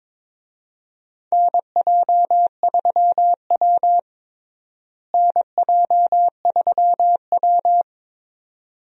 Morse code, heard as a keyed tone, about 22 wpm, 700 Hz. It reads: NJ3W NJ3W
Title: USA Call Signs - 4 Characters - Repeated 2x - 22wpm